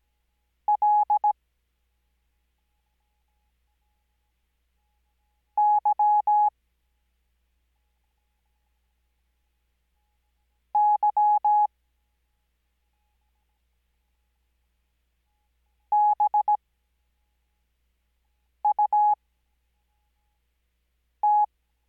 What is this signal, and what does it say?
Morse code, heard as a keyed tone, about 17 wpm, spaced as for 4 wpm, 840 Hz. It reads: L Y Y BUT
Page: Farnsworth Spacing: 15WPM Character speed Spaced at 5WPM.